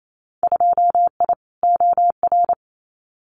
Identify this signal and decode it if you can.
Morse code, heard as a keyed tone, about 28 wpm, 700 Hz. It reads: 2I OR
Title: Winter Field Day Exchanges Encore 28wpm